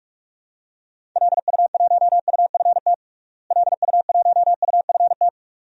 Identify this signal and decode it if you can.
Morse code, heard as a keyed tone, about 46 wpm, 700 Hz. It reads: LU1UFT LU1UFT
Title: DXCC-100 Call Signs - 6 Characters - Repeated 2x - 45wpm